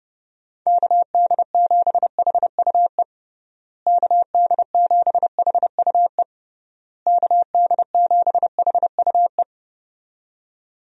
Morse code, heard as a keyed tone, about 30 wpm, 700 Hz. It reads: KD7HUE KD7HUE KD7HUE
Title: USA Call Signs - 6 Characters - Repeated 3x - 30wpm